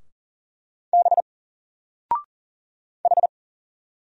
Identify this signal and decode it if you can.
Morse code, heard as a keyed tone, about 39 wpm, 700 Hz. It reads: B H
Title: Single Character (Letter:Number:Punctuation:Prosign) 40wpm